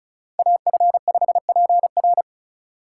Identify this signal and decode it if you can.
Morse code, heard as a keyed tone, about 36 wpm, 700 Hz. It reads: AF5PR
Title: USA Call Signs - 5 Characters - 3x Character Spacing - 35wpm